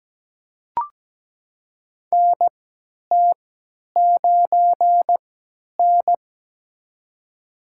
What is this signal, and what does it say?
Morse code, heard as a keyed tone, about 17 wpm, spaced as for 10 wpm, 700 Hz. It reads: NT9N
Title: USA Call Signs - 3x Character Spacing - 17wpm